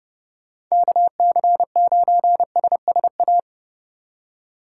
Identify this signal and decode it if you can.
Morse code, heard as a keyed tone, about 35 wpm, 700 Hz. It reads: KC9SSA
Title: Speed Racing - USA Call Signs 30wpm